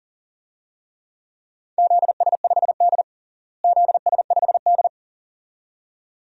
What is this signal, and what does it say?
Morse code, heard as a keyed tone, about 37 wpm, 700 Hz. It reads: ZS5D ZS5D
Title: DXCC-100 Call Signs - 4 Characters - Repeated 2x - 40wpm